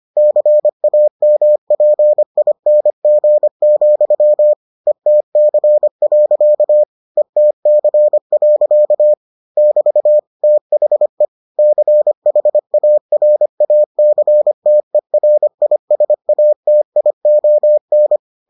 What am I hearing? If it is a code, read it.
Morse code, heard as a keyed tone, about 25 wpm, 600 Hz. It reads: CAMPING, ETC. ETC. <BT> THE CHARACTERISATION